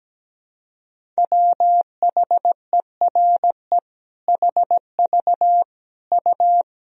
Morse code, heard as a keyed tone, about 17 wpm, 700 Hz. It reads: WHERE HV U
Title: Sentences from Top 300 Words and Abbreviations 17wpm